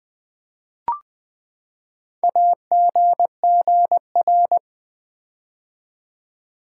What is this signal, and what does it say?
Morse code, heard as a keyed tone, about 20 wpm, 700 Hz. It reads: AGGR
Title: Build A Word - 4 Letters At A Time - 10-Letter Words - 20wpm